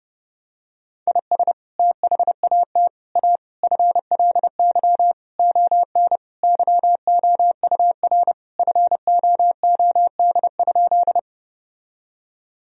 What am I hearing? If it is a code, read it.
Morse code, heard as a keyed tone, about 32 wpm, 700 Hz. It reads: IS THAT A FLY ON YOUR FOOD?
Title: Sentences from Top 1000 Words - Review Part 2 - 30wpm